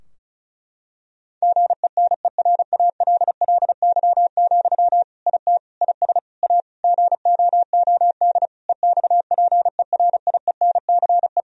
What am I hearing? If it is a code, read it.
Morse code, heard as a keyed tone, about 36 wpm, 700 Hz. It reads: GENERALLY, IT IS A GOOD EXPERIENCE
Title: Sentences from Top 1000 Words - Review Part 4 - 35wpm